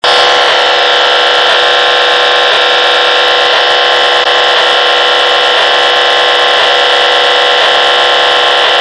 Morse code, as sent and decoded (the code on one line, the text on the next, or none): none